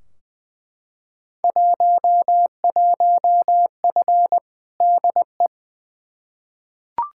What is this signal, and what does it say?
Morse code, heard as a keyed tone, about 20 wpm, 700 Hz. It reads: 11F DE
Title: Field Day Exchanges Encore - 20wpm